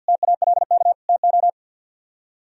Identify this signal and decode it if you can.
Morse code, heard as a keyed tone, about 68 wpm, spaced as for 58 wpm, 700 Hz. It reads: TALK TO
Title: Speed-Racing - Sets of 3 Words 50wpm